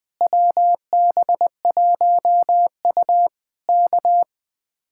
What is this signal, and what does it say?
Morse code, heard as a keyed tone, about 20 wpm, 700 Hz. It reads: WB1U K
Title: Speed-Racing Common QSO Phrases - 20wpm